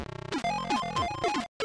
There is no code.